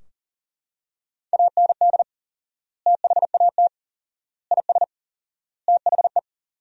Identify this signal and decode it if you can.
Morse code, heard as a keyed tone, about 40 wpm, 700 Hz. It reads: AND THAT IS THE
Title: 4x Word-Spacing - Sentences from Top 400 Words - 40wpm